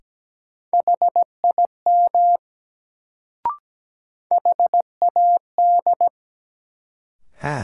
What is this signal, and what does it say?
Morse code, heard as a keyed tone, about 17 wpm, 700 Hz. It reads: HIM HAD
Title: Sets of 2 Words - New Word Drill - 17wpm